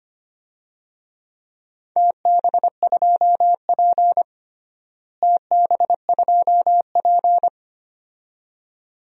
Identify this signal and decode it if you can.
Morse code, heard as a keyed tone, about 25 wpm, 700 Hz. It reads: TB2P TB2P
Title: DXCC-100 Call Signs - 4 Characters - Repeated 2x - 25wpm